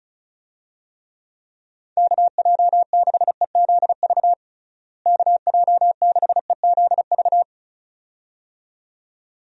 Morse code, heard as a keyed tone, about 34 wpm, 700 Hz. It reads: KJ6EZV KJ6EZV
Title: USA Call Signs - 6 Characters - Repeated 2x - 35wpm